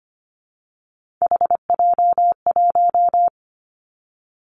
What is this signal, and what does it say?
Morse code, heard as a keyed tone, about 25 wpm, 700 Hz. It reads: HJ1
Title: Three Challenging Letter-Number Combinations 25wpm